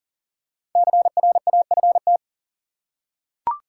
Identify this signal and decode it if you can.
Morse code, heard as a keyed tone, about 38 wpm, 700 Hz. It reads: CRAFT
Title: Five Letter Words - 40wpm